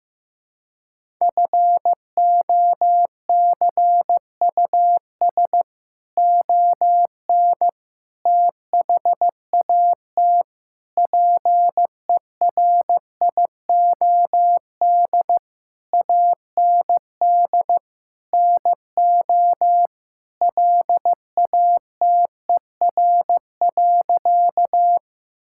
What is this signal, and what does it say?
Morse code, heard as a keyed tone, about 15 wpm, 700 Hz. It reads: FOCUS ON THAT PERIOD AND NO LATER.
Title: Sentences from Top 1000 Words - Review Part 5 - 15wpm